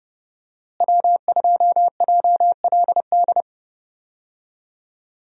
Morse code, heard as a keyed tone, about 30 wpm, 700 Hz. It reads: W2JLD
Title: USA Call Signs - 5 Characters - Repeated 2x - 30wpm